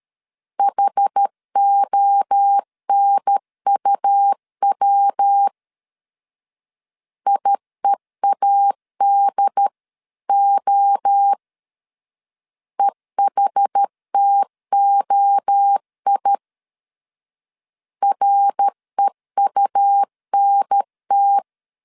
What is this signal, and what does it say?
Morse code, heard as a keyed tone, about 13 wpm, 800 Hz. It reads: HONUW IEADO EHTOI REUNT
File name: CW / DET / 1 3 1500C S C 12wpm 2750